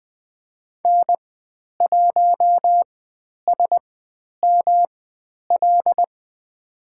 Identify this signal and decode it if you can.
Morse code, heard as a keyed tone, about 20 wpm, spaced as for 10 wpm, 700 Hz. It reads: N1SML
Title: 109 Lesson 18g - Letter m - Call Signs 20wpm